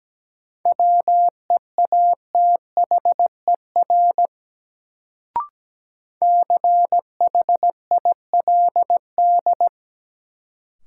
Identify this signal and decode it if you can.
Morse code, heard as a keyed tone, about 17 wpm, 700 Hz. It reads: WEATHER CHILD
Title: Sentences from Top 800 Words - New Word Drill - 17wpm